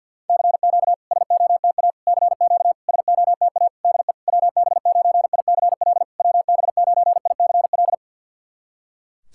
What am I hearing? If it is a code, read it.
Morse code, heard as a keyed tone, about 50 wpm, 700 Hz. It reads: CQ SOTA CQ SOTA DE WB9ICL WB9ICL
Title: SOTA QSO Practice 50wpm